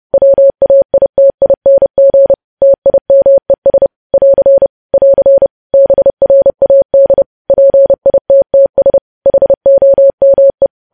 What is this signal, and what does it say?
Morse code, heard as a keyed tone, about 30 wpm, 550 Hz. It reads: WAITING TIMES <AR> <AR> BRAD PITTS HOME
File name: podcast / News headlines-1x-30x30